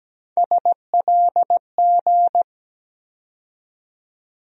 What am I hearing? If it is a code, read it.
Morse code, heard as a keyed tone, about 17 wpm, 700 Hz. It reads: SLG
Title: USA Call Sign Postfixes - 3 Letters 17wpm